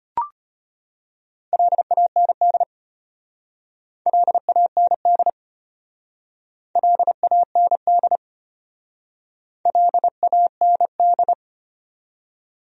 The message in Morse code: LAND LAND LAND LAND

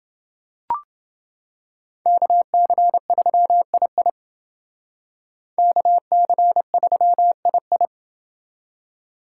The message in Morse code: KC3II KC3II

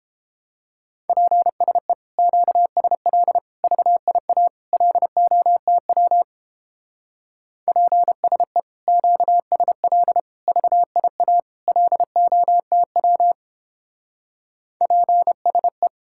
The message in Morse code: PSE QSL VIA LOTW PSE QSL VIA LOTW PSE

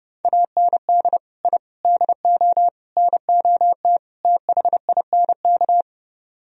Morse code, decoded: AND I DO NOT THINK

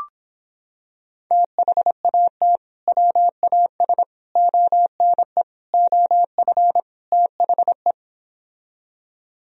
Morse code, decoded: THAT WAS ONE OF THE

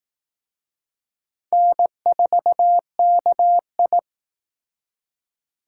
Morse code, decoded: N4KI

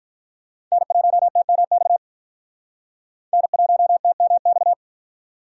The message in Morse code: N1TMX N1TMX

